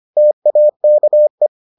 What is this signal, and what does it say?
Morse code, heard as a keyed tone, about 25 wpm, 600 Hz. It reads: TAKE